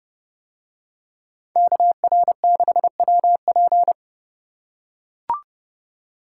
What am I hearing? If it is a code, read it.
Morse code, heard as a keyed tone, about 30 wpm, 700 Hz. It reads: KR6WP